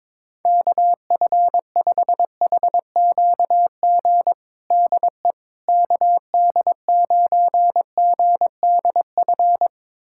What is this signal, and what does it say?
Morse code, heard as a keyed tone, about 22 wpm, 700 Hz. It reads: KF5HQG DE KD9GDF
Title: USA Callsign DE Callsign 22wpm